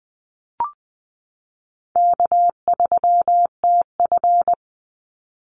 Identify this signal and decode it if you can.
Morse code, heard as a keyed tone, about 20 wpm, 700 Hz. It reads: K3TF
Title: Easy Call Signs Repeated 2x - 20wpm